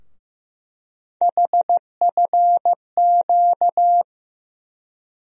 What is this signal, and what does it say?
Morse code, heard as a keyed tone, about 15 wpm, 700 Hz. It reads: HFQ
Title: USA Call Sign Postfixes - 3 Letters 15wpm